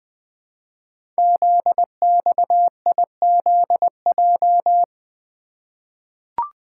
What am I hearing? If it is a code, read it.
Morse code, heard as a keyed tone, about 20 wpm, 700 Hz. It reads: ZXIZJ
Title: Code Groups - 5 Letters - 20wpm